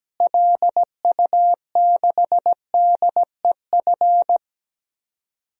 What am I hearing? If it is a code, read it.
Morse code, heard as a keyed tone, about 17 wpm, 700 Hz. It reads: LU6DEF
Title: DXCC-100 Call Signs - 6 Characters - Repeated 2x - 17wpm